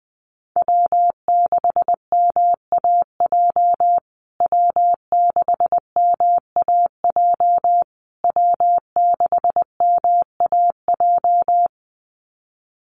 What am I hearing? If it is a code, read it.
Morse code, heard as a keyed tone, about 19 wpm, 700 Hz. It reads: W6MAJ W6MAJ W6MAJ
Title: USA Call Signs Repeated 3x 20wpm